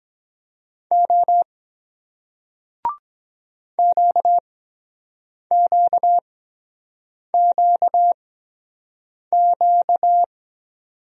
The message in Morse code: O QQQQ

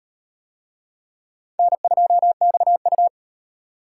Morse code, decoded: N2XU

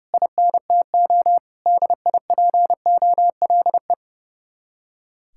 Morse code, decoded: INTO DIPOLE

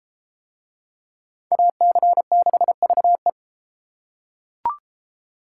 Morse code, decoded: AC6VE

